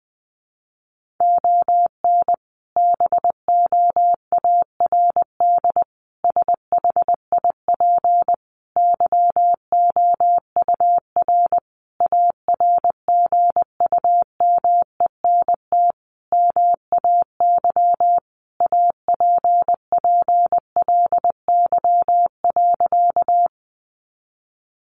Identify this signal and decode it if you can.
Morse code, heard as a keyed tone, about 20 wpm, 700 Hz. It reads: ON BOARD SHIP YOUR ARGUMENT MAY APPLY.
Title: Sentences from Top 1400 Words 20wpm